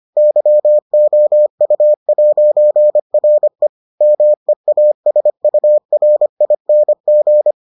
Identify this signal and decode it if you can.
Morse code, heard as a keyed tone, about 25 wpm, 600 Hz. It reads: YOU'RE MEASURING